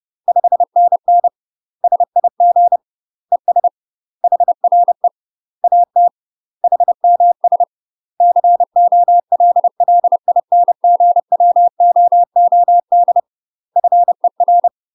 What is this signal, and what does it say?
Morse code, heard as a keyed tone, about 30 wpm, 700 Hz. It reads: 5NN SIG ES HRE AT HMS COLLINGWOOD FER
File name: UK MASTER QSOs TRAINING / 30WPM / UK MASTER QSOs 0019